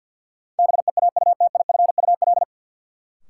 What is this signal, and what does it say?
Morse code, heard as a keyed tone, about 53 wpm, 700 Hz. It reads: BEAUTIFUL
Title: Sentences from Top 900 Words - New Word Drill - 50wpm